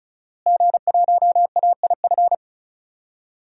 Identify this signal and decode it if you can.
Morse code, heard as a keyed tone, about 34 wpm, 700 Hz. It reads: G1AIF
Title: DXCC-100 Call Signs - 5 Characters - Repeated 3x - 35wpm